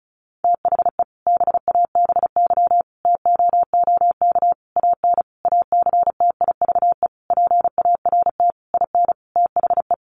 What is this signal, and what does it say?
Morse code, heard as a keyed tone, about 34 wpm, 700 Hz. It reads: THE BABY TOOK AN ACTIVE PART IN THE
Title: Sentences from Top 1300 Words 35wpm